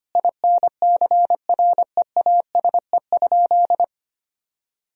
Morse code, heard as a keyed tone, about 25 wpm, 700 Hz. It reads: INCREASE?
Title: Sentences from Top 1000 Words - Review Part 4 - 25wpm